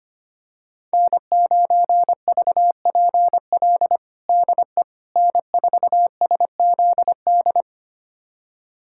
Morse code, heard as a keyed tone, about 25 wpm, 700 Hz. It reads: N9VPL DE N4SZD
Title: USA Callsign DE Callsign - 5 Characters - 25wpm